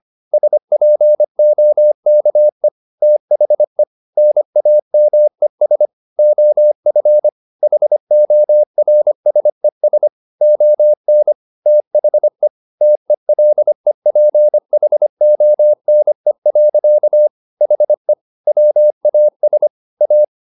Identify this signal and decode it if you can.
Morse code, heard as a keyed tone, about 25 wpm, 600 Hz. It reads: SPOKE THE NAMES OF HORSES ON THE TELEPHONE. HE WAS A
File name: the old man and the sea part 1 25wpm